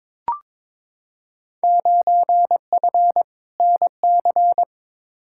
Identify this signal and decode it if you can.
Morse code, heard as a keyed tone, about 23 wpm, 700 Hz. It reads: 9F NC
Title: Field Day Exchanges Encore - 22wpm